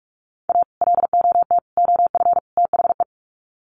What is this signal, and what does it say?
Morse code, heard as a keyed tone, about 44 wpm, 700 Hz. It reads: A LOT OF THE
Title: Sets of 4 Words 45wpm